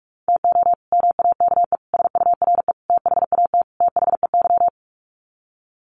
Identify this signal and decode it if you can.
Morse code, heard as a keyed tone, about 44 wpm, 700 Hz. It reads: TO MAKE SURE THAT THEY
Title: Sets of 5 Words Encore2 45wpm